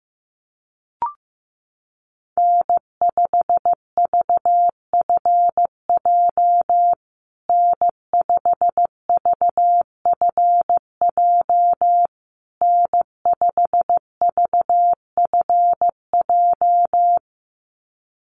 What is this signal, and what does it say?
Morse code, heard as a keyed tone, about 15 wpm, 700 Hz. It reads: N5VFJ N5VFJ N5VFJ
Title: USA Call Signs Repeated 3x 15wpm